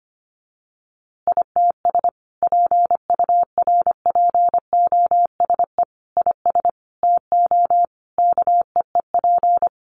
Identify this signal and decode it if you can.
Morse code, heard as a keyed tone, about 25 wpm, 700 Hz. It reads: ITS PURPOSE IS TO KEEP
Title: A Sound Of Thunder 25wpm